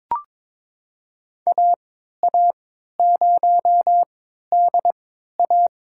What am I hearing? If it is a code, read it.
Morse code, heard as a keyed tone, about 21 wpm, spaced as for 12 wpm, 700 Hz. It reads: AA0DA
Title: USA Call Signs - 5 Characters - 3x Character Spacing - 22wpm